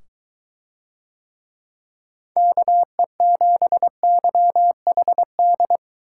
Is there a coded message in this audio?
Yes: Morse code from 2.4 to 5.8 s, heard as a keyed tone, about 21 wpm, 700 Hz. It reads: KE7YHD